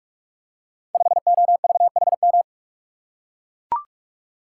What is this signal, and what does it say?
Morse code, heard as a keyed tone, about 41 wpm, 700 Hz. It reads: 5OVHM
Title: Code Groups - 5 Letter:Numbers - 45wpm